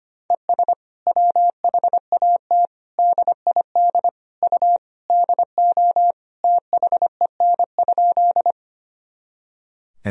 Morse code, heard as a keyed tone, about 25 wpm, 700 Hz. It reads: ES WHAT DID U DO THEN?
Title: Sentences from Top 100 Words and Abbreviations 25wpm